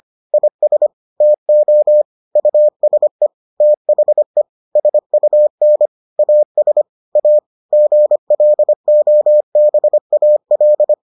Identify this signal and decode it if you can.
Morse code, heard as a keyed tone, about 25 wpm, 600 Hz. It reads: IS TO USE THE SUN AS A GLOBAL